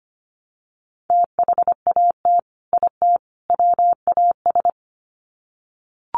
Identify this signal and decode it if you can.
Morse code, heard as a keyed tone, about 25 wpm, 700 Hz. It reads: THAT IT WAS
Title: Sets of 3 Words 25wpm